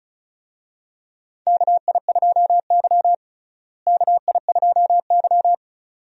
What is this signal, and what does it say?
Morse code, heard as a keyed tone, about 36 wpm, 700 Hz. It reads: KI2Y KI2Y
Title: USA Call Signs - 4 Characters - Repeated 2x - 35wpm